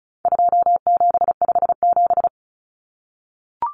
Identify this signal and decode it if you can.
Morse code, heard as a keyed tone, about 34 wpm, 700 Hz. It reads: 2757
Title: Common Cut Numbers - 4-Digit 35wpm